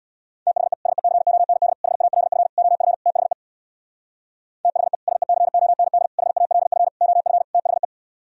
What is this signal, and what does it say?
Morse code, heard as a keyed tone, about 62 wpm, 700 Hz. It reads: THE SECOND HALF OF THE THE SECOND HALF OF THE
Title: Speed-Racing - Sets of 5 Words 50wpm